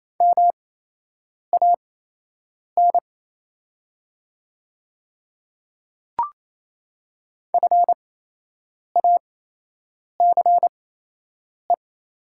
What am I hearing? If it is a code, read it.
Morse code, heard as a keyed tone, about 28 wpm, spaced as for 8 wpm, 690 Hz. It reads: MAN FACE
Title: 8x Character Spacing - Top 500 Words - 28wpm